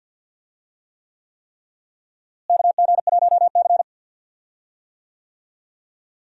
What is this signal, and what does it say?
Morse code, heard as a keyed tone, about 52 wpm, 700 Hz. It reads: KG1C
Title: USA Call Signs - 4 Characters - Repeated 3x - 50wpm